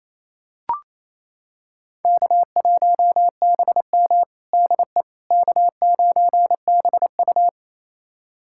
Morse code, heard as a keyed tone, about 28 wpm, 700 Hz. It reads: K1BM DE K9BU
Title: USA Callsign DE Callsign - 4 Characters - 28wpm